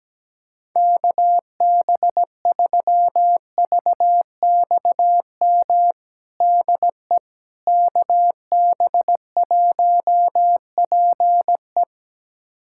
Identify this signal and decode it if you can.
Morse code, heard as a keyed tone, about 17 wpm, 700 Hz. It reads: KB3VXM DE KB1PE
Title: USA Callsign DE Callsign 17wpm